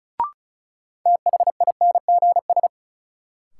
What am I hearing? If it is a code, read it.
Morse code, heard as a keyed tone, about 36 wpm, 700 Hz. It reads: THINGS
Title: Sentences from Top 1000 Words - Review Part 4 - New Word Drill - 35wpm